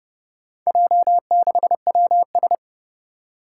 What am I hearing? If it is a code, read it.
Morse code, heard as a keyed tone, about 30 wpm, 700 Hz. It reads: J6WS
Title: Four Challenging Letter-Number Combinations 30wpm